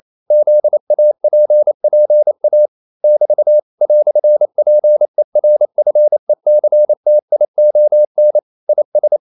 Morse code, heard as a keyed tone, about 28 wpm, 600 Hz. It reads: ZAPPA <BT> "PERFECTION IS
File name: CW / Practice / EnglishQuotations0228